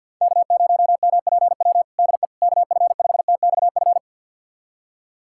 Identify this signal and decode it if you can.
Morse code, heard as a keyed tone, about 46 wpm, 700 Hz. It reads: K0MPW DE KF5TXF
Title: USA Callsign DE Callsign 50wpm